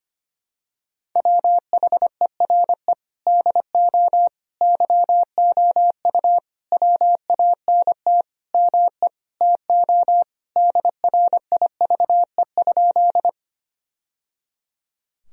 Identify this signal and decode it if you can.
Morse code, heard as a keyed tone, about 25 wpm, 700 Hz. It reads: WHERE DO YOU WANT ME TO DRIVE?
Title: Sentences from Top 1000 Words - Review Part 2 - 25wpm